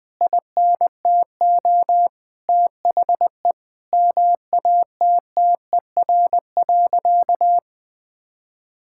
Morse code, heard as a keyed tone, about 20 wpm, 700 Hz. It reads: INTO THE MATTER.
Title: Sentences from Top 1000 Words - Review Part 6 - 20wpm